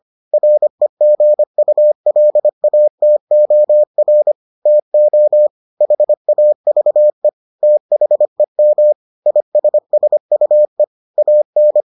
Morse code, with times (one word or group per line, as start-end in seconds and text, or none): 0.3-4.3 REGULATOR
4.7-5.5 TO
5.8-7.3 HAVE
7.6-8.9 THEM
9.3-10.8 ISSUE
11.2-11.8 AN